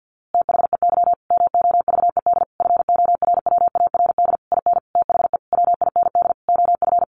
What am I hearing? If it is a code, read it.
Morse code, heard as a keyed tone, about 50 wpm, 700 Hz. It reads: THEY MOVED FORWARD IN THE WIND OF
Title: A Sound Of Thunder 50wpm